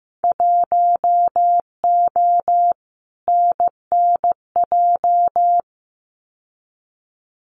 Morse code, heard as a keyed tone, about 15 wpm, 700 Hz. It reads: 1O NNJ
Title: Winter Field Day Exchanges 15wpm